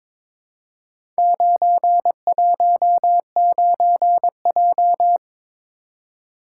Morse code, heard as a keyed tone, about 21 wpm, 700 Hz. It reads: 919J
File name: practice / 22wpm / Four Letter-Number Combinations - 22wpm